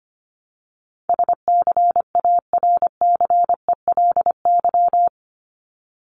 Morse code, heard as a keyed tone, about 25 wpm, 700 Hz. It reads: SCARCELY
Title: LY Suffix - Top 100 Words with Base Words 25wpm